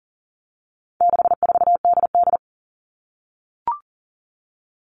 Morse code, heard as a keyed tone, about 40 wpm, 700 Hz. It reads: 64DD
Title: All Cut Numbers - 4-Digit 40wpm